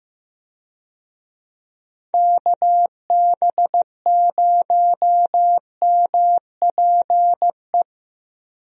Morse code, heard as a keyed tone, about 16 wpm, 700 Hz. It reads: KB0MPE
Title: USA Call Signs - 6 Characters - 15wpm